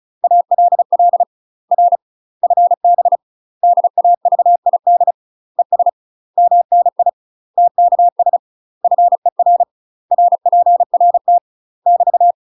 Morse code, heard as a keyed tone, about 36 wpm, 700 Hz. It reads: ALL R FB DAVID ES MNI TKS FER RPRT <BT>
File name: UK MASTER QSOs TRAINING/35WPM